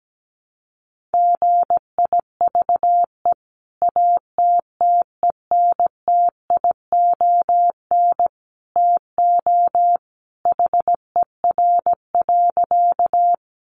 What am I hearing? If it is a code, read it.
Morse code, heard as a keyed tone, about 17 wpm, 700 Hz. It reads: GIVE ATTENTION TO HER.
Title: Sentences from Top 1000 Words 17wpm